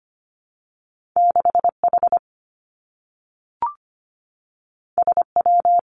Morse code, heard as a keyed tone, about 25 wpm, 700 Hz. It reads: 6 H S W
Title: Two Challenging Letter-Number Combinations 25wpm